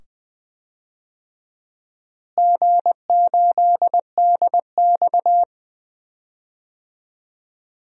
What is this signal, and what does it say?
Morse code, heard as a keyed tone, about 20 wpm, 700 Hz. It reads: G8DX
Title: DXCC-100 Call Signs - 4 Characters - Repeated 2x - 20wpm